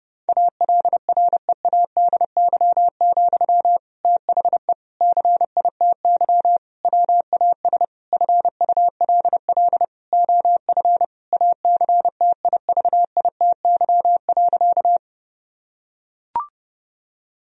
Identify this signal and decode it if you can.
Morse code, heard as a keyed tone, about 32 wpm, 700 Hz. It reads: ALREADY, THE CITY WAS FULL OF ACTIVITY.
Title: Sentences from Top 1000 Words - Review Part 4 - 30wpm